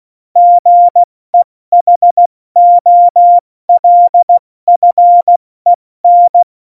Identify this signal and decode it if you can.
Morse code, heard as a keyed tone, about 16 wpm, spaced as for 14 wpm, 700 Hz. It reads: GEHOLFEN
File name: Morsekurs / Witze / Witz 30. April 2019, 70 F80